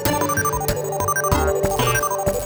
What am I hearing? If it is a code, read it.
There is no code.